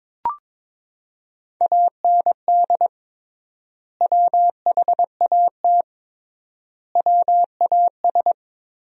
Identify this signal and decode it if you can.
Morse code, heard as a keyed tone, about 22 wpm, 700 Hz. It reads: AND WHAT WAS
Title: 3x Word-Spacing - Sentences from Top 200 Words - 22wpm